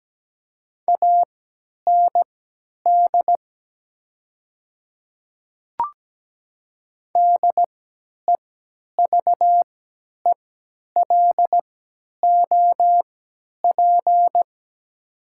Morse code, heard as a keyed tone, about 17 wpm, spaced as for 10 wpm, 700 Hz. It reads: AND DEVELOP
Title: 3x Character Spacing - Top 500 Words - 17wpm